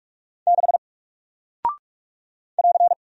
Morse code, heard as a keyed tone, about 44 wpm, 700 Hz. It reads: 6 <AR>
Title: Single Character (Letter:Number:Punctuation:Prosign) Alt 45wpm